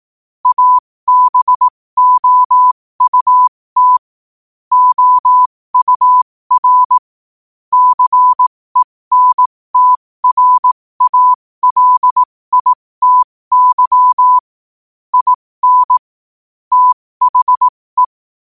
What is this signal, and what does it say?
Morse code, heard as a keyed tone, about 18 wpm, spaced as for 15 wpm, 1000 Hz. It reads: ABOUT OUR CENTRALITY IN THE
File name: qotdfiles / 15wpm